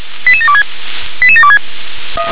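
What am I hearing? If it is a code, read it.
Five-tone selective call (ZVEI), heard as quick runs of five tones, tones of about 70 ms. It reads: 88726 88726
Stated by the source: ZVEI five-tone call received by SDR.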